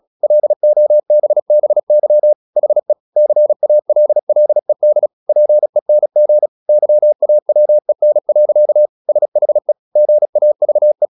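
Morse code, heard as a keyed tone, about 36 wpm, 600 Hz. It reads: LOBBY HE CALLED PENG YAWEN. SHE GAVE
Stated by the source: Morse practice at 36 wpm, text of Ken Follett's novel Never, from CW med Gnister.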